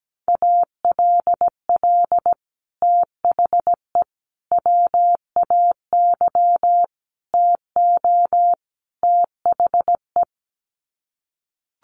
Morse code, heard as a keyed tone, about 17 wpm, 700 Hz. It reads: ALL THE WAY TO THE
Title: Sets of 5 Words 17wpm